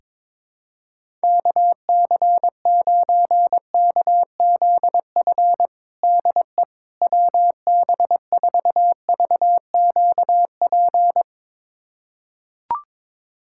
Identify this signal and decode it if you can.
Morse code, heard as a keyed tone, about 22 wpm, 700 Hz. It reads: KC9KZF DE WB4VQP
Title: USA Callsign DE Callsign - 6 Characters - 22wpm